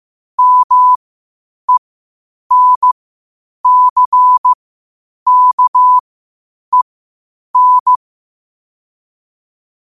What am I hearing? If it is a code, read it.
Morse code, heard as a keyed tone, about 15 wpm, spaced as for 8 wpm, 1000 Hz. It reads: MENCKEN
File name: qotdfiles / 8wpm